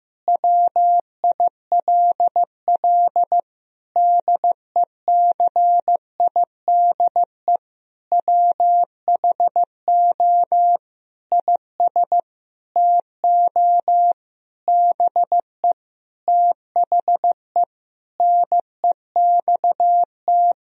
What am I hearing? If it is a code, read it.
Morse code, heard as a keyed tone, about 15 wpm, 700 Hz. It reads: WILL DECIDE WHO IS TO BE THE NEXT
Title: Sentences from Top 1000 Words - Review Part 3 - 15wpm